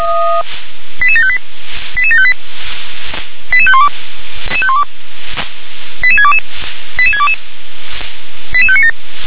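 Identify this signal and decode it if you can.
Five-tone selective call (ZVEI), heard as quick runs of five tones, tones of about 70 ms. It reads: 88758 88758 88512 88622 88622 88597
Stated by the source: ZVEI selective call 88731 received on an SDR.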